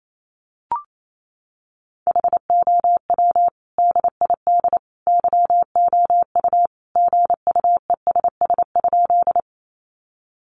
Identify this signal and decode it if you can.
Morse code, heard as a keyed tone, about 28 wpm, 700 Hz. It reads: HOW DID YOU GUESS?
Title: Sentences from Top 1500 Words 28wpm